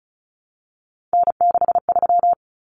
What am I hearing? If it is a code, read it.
Morse code, heard as a keyed tone, about 34 wpm, 700 Hz. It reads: N63
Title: Common Cut Numbers - 3-Digit 35wpm